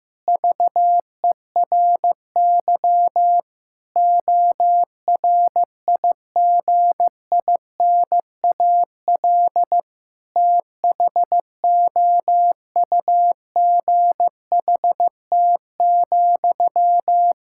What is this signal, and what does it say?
Morse code, heard as a keyed tone, about 15 wpm, 700 Hz. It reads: VERY ORIGINAL THOUGHT,
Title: Sentences from Top 1000 Words - Review Part 6 - 15wpm